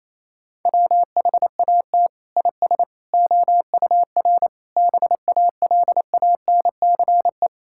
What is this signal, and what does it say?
Morse code, heard as a keyed tone, about 28 wpm, 700 Hz. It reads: WHAT IS OUR BALANCE
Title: Sentences from Top 1000 Words - Review Part 6 - 28wpm